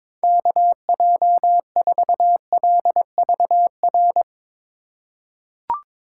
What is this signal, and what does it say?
Morse code, heard as a keyed tone, about 22 wpm, 700 Hz. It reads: KJ4LVR
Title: USA Call Signs - 8x Character Spacing - 22wpm